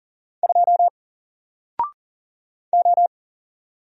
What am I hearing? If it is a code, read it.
Morse code, heard as a keyed tone, about 40 wpm, 700 Hz. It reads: 2 O
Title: Single Character (Letter:Number:Punctuation:Prosign) Alt 40wpm